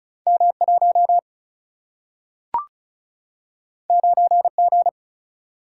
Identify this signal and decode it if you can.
Morse code, heard as a keyed tone, about 36 wpm, 700 Hz. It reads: M 1 9 G
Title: Single-Letter Single-Number 35wpm